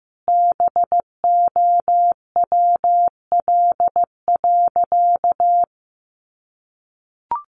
Morse code, heard as a keyed tone, about 15 wpm, 700 Hz. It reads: BOWL.
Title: Sentences from Top 1200 Words 15wpm